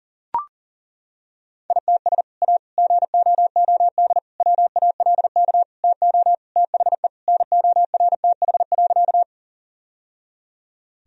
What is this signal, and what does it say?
Morse code, heard as a keyed tone, about 40 wpm, 700 Hz. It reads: ITS A GOOD WALK TO THE NORTH.
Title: Sentences from Top 1000 Words - Review Part 1 - 40wpm